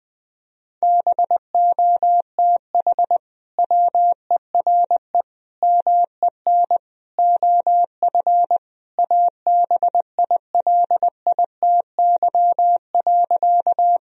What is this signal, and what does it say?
Morse code, heard as a keyed tone, about 20 wpm, 700 Hz. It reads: BOTH WERE MEN OF ABILITY.
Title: Sentences from Top 1000 Words - Review Part 3 - 20wpm